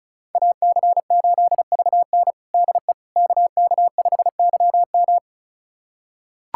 Morse code, heard as a keyed tone, about 36 wpm, 700 Hz. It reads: AC8VN DE KK5YM
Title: USA Callsign DE Callsign - 5 Characters - 35wpm